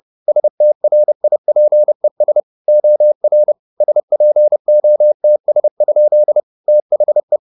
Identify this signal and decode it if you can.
Morse code, heard as a keyed tone, about 30 wpm, 600 Hz. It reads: STRIPES OR SPOTS? THE
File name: the old man and the sea 2 30wpm